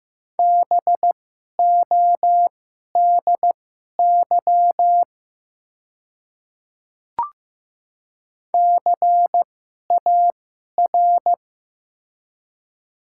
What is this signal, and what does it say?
Morse code, heard as a keyed tone, about 15 wpm, spaced as for 11 wpm, 700 Hz. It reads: BODY CAR
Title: 2x Character Spacing - Top 500 Words - 15wpm